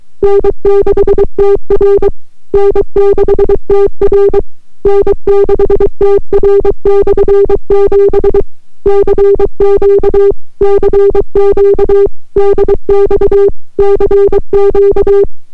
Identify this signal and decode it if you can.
Morse code, heard as a keyed tone, about 23 wpm, 400 Hz. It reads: N6TR N6TR N6TR/7 CQ CQ DX CQ